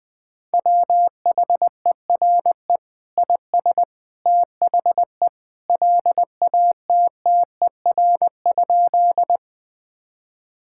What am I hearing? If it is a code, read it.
Morse code, heard as a keyed tone, about 20 wpm, 700 Hz. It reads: WHERE IS THE LATTER?
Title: Sentences from Top 1600 Words 20wpm